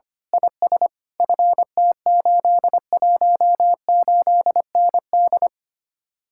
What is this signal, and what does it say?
Morse code, heard as a keyed tone, about 24 wpm, 700 Hz. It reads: IS FT818ND
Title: Speed-Racing Common QSO Phrases - 25wpm